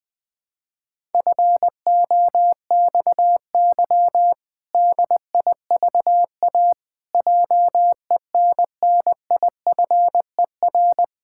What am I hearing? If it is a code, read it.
Morse code, heard as a keyed tone, about 20 wpm, 700 Hz. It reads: FOXY DIVA JENNIFER
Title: Pangram Challenge v2 - 20wpm